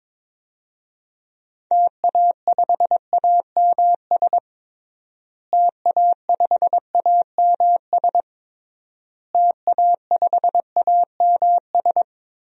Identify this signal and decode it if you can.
Morse code, heard as a keyed tone, about 22 wpm, 700 Hz. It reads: TA5AMS TA5AMS TA5AMS
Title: DXCC-100 Call Signs - 6 Characters - Repeated 3x - 22wpm